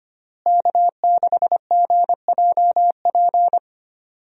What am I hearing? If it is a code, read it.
Morse code, heard as a keyed tone, about 25 wpm, 700 Hz. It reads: K6GJP
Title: USA Call Signs - 5 Characters - Repeated 2x - 25wpm